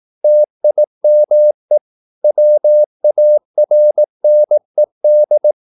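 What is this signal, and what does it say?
Morse code, heard as a keyed tone, about 18 wpm, 600 Hz. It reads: TIME WARNED
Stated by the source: Morse practice at 18 wpm, English text, from CW med Gnister.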